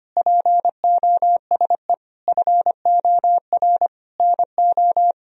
Morse code, heard as a keyed tone, about 25 wpm, 700 Hz. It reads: POSE FOR NO
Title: Sentences from Top 1000 Words - Review Part 2 - 25wpm